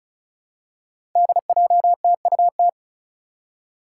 Morse code, heard as a keyed tone, about 34 wpm, 700 Hz. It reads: DJTUT
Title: Five Challenging Letter-Number Combinations 35wpm